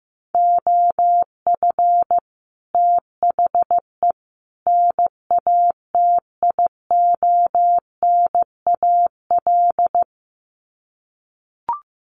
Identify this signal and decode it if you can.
Morse code, heard as a keyed tone, about 15 wpm, 700 Hz. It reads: OF THE NATIONAL
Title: Sets of 3 Words 15wpm